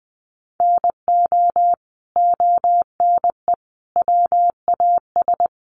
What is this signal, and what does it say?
Morse code, heard as a keyed tone, about 20 wpm, 700 Hz. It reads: NO ONE WAS
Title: Sentences from Top 1200 Words 20wpm